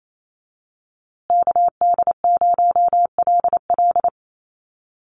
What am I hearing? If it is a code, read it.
Morse code, heard as a keyed tone, about 28 wpm, 700 Hz. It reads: KD0LL
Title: USA Call Signs Repeated 3x 28wpm